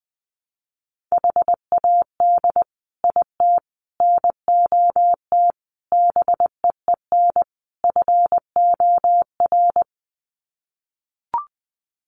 Morse code, heard as a keyed tone, about 20 wpm, 700 Hz. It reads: HAD IT NOT BEEN FOR
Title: Sets of 5 Words Encore2 20wpm